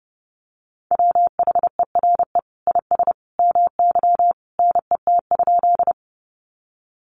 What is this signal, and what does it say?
Morse code, heard as a keyed tone, about 30 wpm, 700 Hz. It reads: WHERE IS MY NET?
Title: Sentences from Top 1400 Words 30wpm